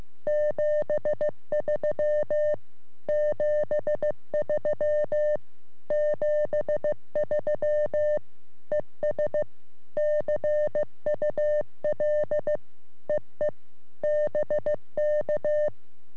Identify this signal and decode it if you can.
Morse code, heard as a keyed tone, about 15 wpm, 600 Hz. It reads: 73 73 73 ES CUL EE BK
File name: sm7lcb / g3xdy 3cm